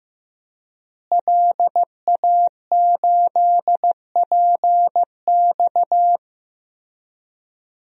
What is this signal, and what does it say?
Morse code, heard as a keyed tone, about 16 wpm, 700 Hz. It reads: LA8PX
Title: DXCC-100 Call Signs - 5 Characters - Repeated 3x - 15wpm